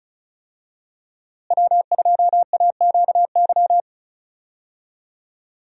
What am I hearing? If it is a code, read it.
Morse code, heard as a keyed tone, about 36 wpm, 700 Hz. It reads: W2AQY
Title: USA Call Signs - 5 Characters - Repeated 3x - 35wpm